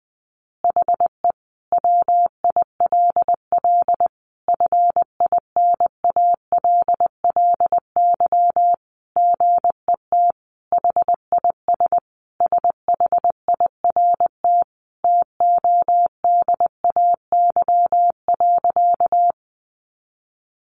Morse code, heard as a keyed tone, about 20 wpm, 700 Hz. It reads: HE WILL FINALLY GET HIS SHIRT TODAY.
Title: Additional Easy Phrases 20wpm